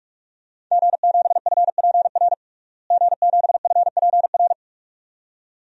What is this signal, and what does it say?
Morse code, heard as a keyed tone, about 46 wpm, 700 Hz. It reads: G7FPR G7FPR
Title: DXCC-100 Call Signs - 5 Characters - Repeated 2x - 45wpm